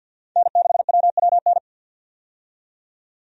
Morse code, heard as a keyed tone, about 47 wpm, 700 Hz. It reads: N6WWN
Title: USA Call Signs - 5 Characters - 50wpm